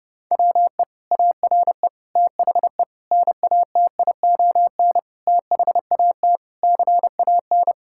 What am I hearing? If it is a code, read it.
Morse code, heard as a keyed tone, about 30 wpm, 700 Hz. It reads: WE ARE THE NATION THAT CAN
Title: Sentences from Top 1000 Words - Review Part 6 - 30wpm